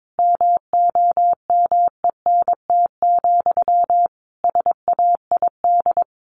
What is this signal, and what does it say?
Morse code, heard as a keyed tone, about 22 wpm, 700 Hz. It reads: MOMENT, SAID
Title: Sentences from Top 1000 Words 22wpm